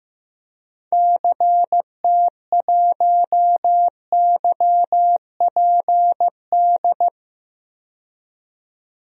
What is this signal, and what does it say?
Morse code, heard as a keyed tone, about 15 wpm, 700 Hz. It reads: CT1YPD
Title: DXCC-100 Call Signs - 6 Characters - 15wpm